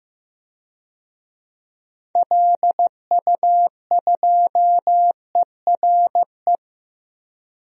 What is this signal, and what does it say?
Morse code, heard as a keyed tone, about 15 wpm, 700 Hz. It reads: LU2ERE
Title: DXCC-100 Call Signs - 6 Characters - 15wpm